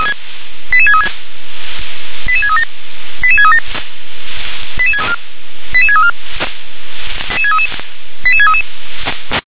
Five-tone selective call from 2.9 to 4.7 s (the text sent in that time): 88637